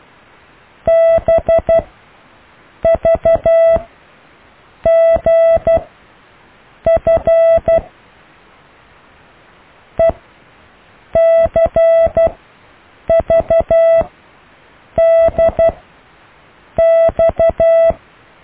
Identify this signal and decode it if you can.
Morse code, heard as a keyed tone, about 12 wpm, spaced as for 6 wpm, 660 Hz. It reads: BVGF ECVDX
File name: cw / Alle letters